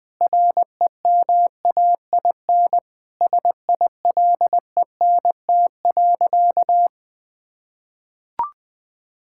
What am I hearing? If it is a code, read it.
Morse code, heard as a keyed tone, about 20 wpm, 700 Hz. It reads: REMAIN SILENT.
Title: Sentences with 4-Word Phrases - Spoken Beginning - 20wpm